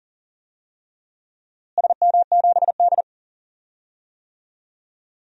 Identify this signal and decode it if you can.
Morse code, heard as a keyed tone, about 37 wpm, 700 Hz. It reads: SM7D
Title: DXCC-100 Call Signs - 4 Characters - Repeated 2x - 40wpm